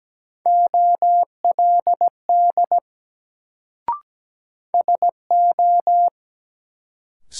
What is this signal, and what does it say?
Morse code, heard as a keyed tone, about 17 wpm, 700 Hz. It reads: OLD SO
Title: Sets of 2 Words - New Word Drill - 17wpm